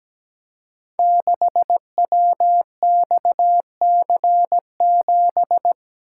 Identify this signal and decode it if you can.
Morse code, heard as a keyed tone, about 17 wpm, 700 Hz. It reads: 6WXC7
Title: Code Groups - 5 Letter:Numbers - 17wpm